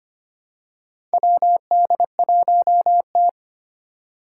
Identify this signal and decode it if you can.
Morse code, heard as a keyed tone, about 25 wpm, 700 Hz. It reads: WD1T
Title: Four Challenging Letter-Number Combinations 25wpm